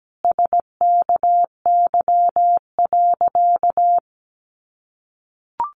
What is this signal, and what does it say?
Morse code, heard as a keyed tone, about 17 wpm, 700 Hz. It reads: SKY.
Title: A Sound Of Thunder 17wpm